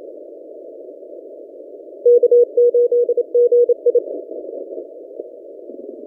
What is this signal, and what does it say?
Morse code, heard as a keyed tone, about 26 wpm, 480 Hz. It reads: K8GI
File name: posts / v63mj-17cw- -0006z